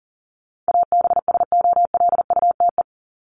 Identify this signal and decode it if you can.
Morse code, heard as a keyed tone, about 40 wpm, 700 Hz. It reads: ABSOLUTE
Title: LY Suffix - Top 100 Words with Base Words 40wpm